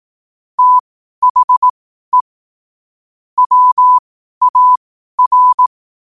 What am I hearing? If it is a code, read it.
Morse code, heard as a keyed tone, about 18 wpm, spaced as for 12 wpm, 1000 Hz. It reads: THE WAR
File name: qotdfiles / 12wpm